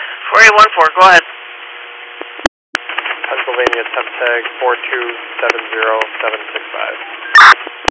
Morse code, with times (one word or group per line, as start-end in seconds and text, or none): none